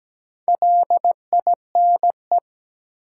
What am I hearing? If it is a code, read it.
Morse code, heard as a keyed tone, about 17 wpm, 700 Hz. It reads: LINE
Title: Sentences from Top 1200 Words - New Word Drill Alt - 17wpm